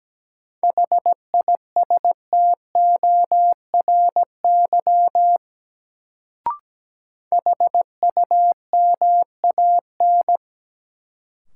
Morse code, heard as a keyed tone, about 17 wpm, 700 Hz. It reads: HISTORY HUMAN
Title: Sentences from Top 600 Words - New Word Drill - 17wpm